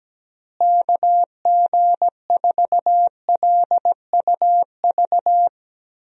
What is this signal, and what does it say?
Morse code, heard as a keyed tone, about 17 wpm, 700 Hz. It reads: KG4LUV
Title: USA Call Signs - 6 Characters - Repeated 3x - 17wpm